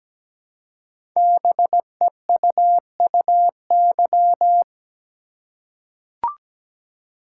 Morse code, heard as a keyed tone, about 17 wpm, 700 Hz. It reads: BEUUY
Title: Code Groups - 5 Letters - 17wpm